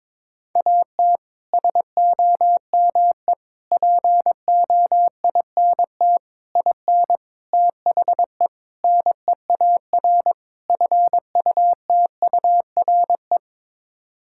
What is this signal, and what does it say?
Morse code, heard as a keyed tone, about 22 wpm, 700 Hz. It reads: AT SOME POINT IN THE NEAR FUTURE
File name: practice / 22wpm / Sets of 7 Words - 22wpm